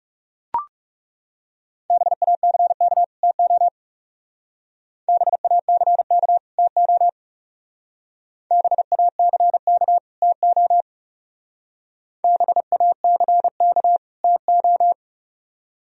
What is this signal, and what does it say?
Morse code, heard as a keyed tone, about 38 wpm, 700 Hz. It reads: BACK TO BACK TO BACK TO BACK TO